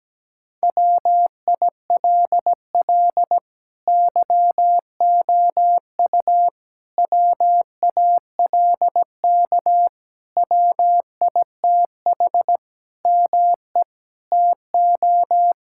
Morse code, heard as a keyed tone, about 17 wpm, 700 Hz. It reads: WILL YOU WALK WITH ME TO
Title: Sentences from Top 1000 Words - Review Part 6 - 17wpm